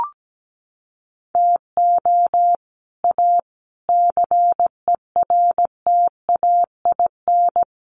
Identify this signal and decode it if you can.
Morse code, heard as a keyed tone, about 17 wpm, 700 Hz. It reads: TO A CERTAIN
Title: Sentences from Top 1500 Words 17wpm